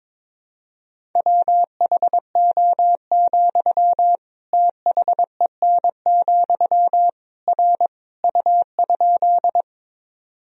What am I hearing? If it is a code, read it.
Morse code, heard as a keyed tone, about 22 wpm, 700 Hz. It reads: WHO, THEN, R U?
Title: Sentences from Top 100 Words and Abbreviations 22wpm